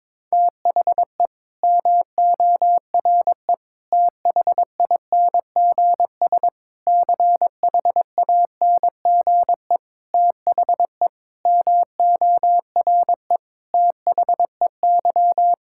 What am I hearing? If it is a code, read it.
Morse code, heard as a keyed tone, about 21 wpm, 700 Hz. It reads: THE MORE THINGS CHANGE THE MORE THEY
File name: Sets of 9 Words - 22wpm